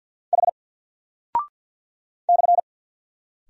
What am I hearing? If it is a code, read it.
Morse code, heard as a keyed tone, about 49 wpm, 700 Hz. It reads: H /
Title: Single Character (Letter:Number:Punctuation:Prosign) Alt 50wpm